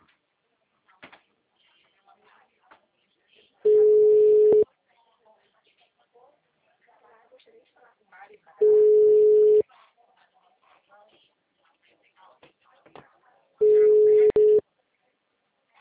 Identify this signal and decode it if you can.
Ringback tone (Europe, 425 Hz), with speech.